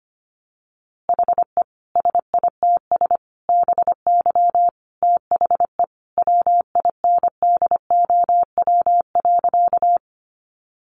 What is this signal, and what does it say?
Morse code, heard as a keyed tone, about 25 wpm, 700 Hz. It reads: HE SITS BY THE WINDOW.
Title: Easy Phrases 25wpm